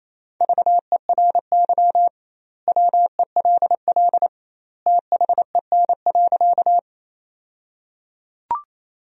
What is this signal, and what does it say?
Morse code, heard as a keyed tone, about 28 wpm, 700 Hz. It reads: VERY WELL THEN.
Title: 2x Word-Spacing - Sentences from Top 200 Words - 28wpm